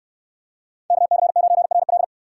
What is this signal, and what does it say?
Morse code, heard as a keyed tone, about 62 wpm, 700 Hz. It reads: KC9RL